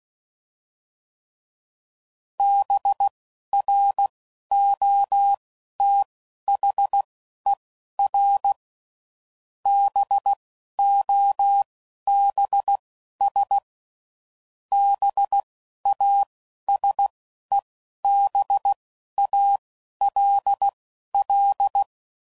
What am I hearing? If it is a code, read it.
Morse code, heard as a keyed tone, about 16 wpm, spaced as for 11 wpm, 800 Hz. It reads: BROTHER BOBS BASEBALL